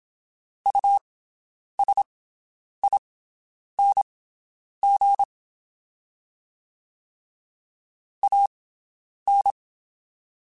Morse code, heard as a keyed tone, about 27 wpm, spaced as for 9 wpm, 800 Hz. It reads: USING AN
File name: QSOs Spaced